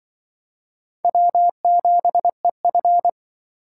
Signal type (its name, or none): Morse code